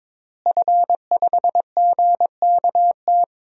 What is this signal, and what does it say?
Morse code, heard as a keyed tone, about 22 wpm, 700 Hz. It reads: F5GKT
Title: DXCC-100 Call Signs - 5 Characters - Repeated 3x - 22wpm